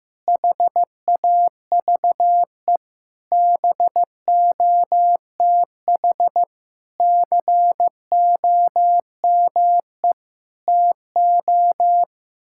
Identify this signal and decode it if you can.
Morse code, heard as a keyed tone, about 15 wpm, 700 Hz. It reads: HAVE BOTH COME TO
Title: Sentences from Top 1600 Words 15wpm